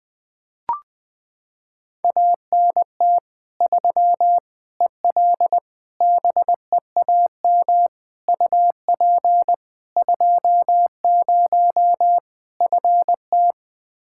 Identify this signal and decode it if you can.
Morse code, heard as a keyed tone, about 21 wpm, 700 Hz. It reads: ANT 3 EL BEAM UP 20 FT
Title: Practice QSO - Exchange 2 - Info - 20wpm